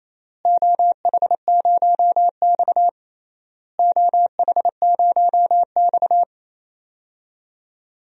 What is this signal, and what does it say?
Morse code, heard as a keyed tone, about 28 wpm, 700 Hz. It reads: OH0X OH0X
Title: DXCC-100 Call Signs - 4 Characters - Repeated 3x - 28wpm